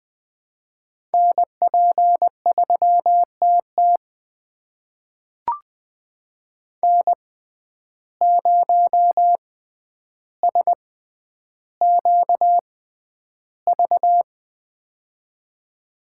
Morse code, heard as a keyed tone, about 21 wpm, spaced as for 10 wpm, 700 Hz. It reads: NP3TT N0SQV
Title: USA Call Signs - 5 Characters - 6x Character Spacing - 20wpm